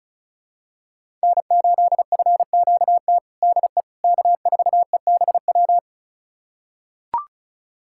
Morse code, heard as a keyed tone, about 36 wpm, 700 Hz. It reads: N8FQT DE K4EBW
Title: USA Callsign DE Callsign - 5 Characters - 35wpm